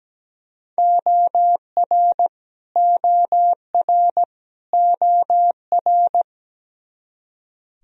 Morse code, heard as a keyed tone, about 17 wpm, 700 Hz. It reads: OR OR OR
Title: Top 100 Words Repeated 17wpm